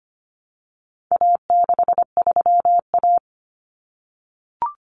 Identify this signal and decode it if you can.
Morse code, heard as a keyed tone, about 25 wpm, 700 Hz. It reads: A63A
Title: Common Cut Numbers - 4-Digit 25wpm